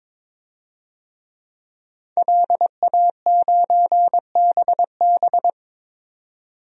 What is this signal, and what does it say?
Morse code, heard as a keyed tone, about 22 wpm, 700 Hz. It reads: LA9BB
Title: DXCC-100 Call Signs - 5 Characters - 22wpm